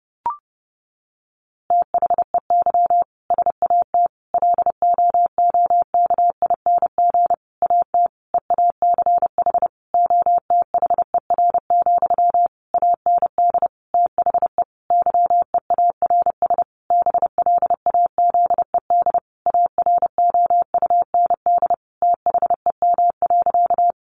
Morse code, heard as a keyed tone, about 30 wpm, 700 Hz. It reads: THEY SAT LOOKING AT EACH OTHER, AND THE YEARS BLAZED AROUND THEM.
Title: A Sound Of Thunder 30wpm